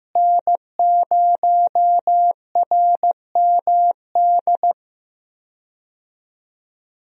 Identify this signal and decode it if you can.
Morse code, heard as a keyed tone, about 15 wpm, 700 Hz. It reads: N0RMD
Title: USA Call Signs - 5 Characters - Repeated 3x - 15wpm